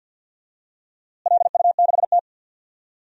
Morse code, heard as a keyed tone, about 48 wpm, 700 Hz. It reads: LU6T
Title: DXCC-100 Call Signs - 4 Characters - Repeated 2x - 50wpm